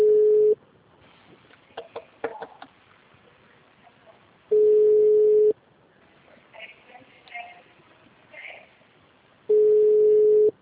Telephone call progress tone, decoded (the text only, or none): ringback tone